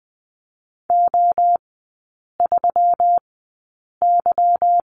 Morse code, heard as a keyed tone, about 20 wpm, spaced as for 8 wpm, 700 Hz. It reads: O3Y
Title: 184 Lesson 30e - Letter b - Three Characters 20wpm